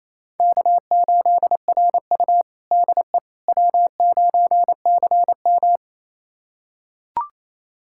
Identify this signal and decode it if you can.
Morse code, heard as a keyed tone, about 28 wpm, 700 Hz. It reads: K8RU DE W9CM
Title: USA Callsign DE Callsign - 4 Characters - 28wpm